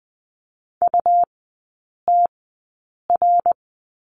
Morse code, heard as a keyed tone, about 20 wpm, spaced as for 8 wpm, 700 Hz. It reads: UTR
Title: 100 Lesson 17e - Letter u - Three Characters 20wpm